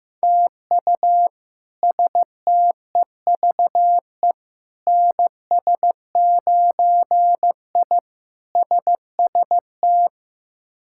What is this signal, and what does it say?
Morse code, heard as a keyed tone, about 15 wpm, 700 Hz. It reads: TU STEVE NS9I SST
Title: K1USN SST QSO Practice - 15wpm